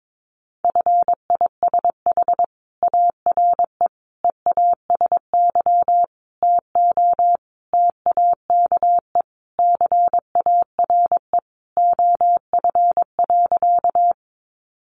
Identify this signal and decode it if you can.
Morse code, heard as a keyed tone, about 22 wpm, 700 Hz. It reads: FISH ARE EASY TO TAKE CARE OF.
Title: Additional Easy Phrases 22wpm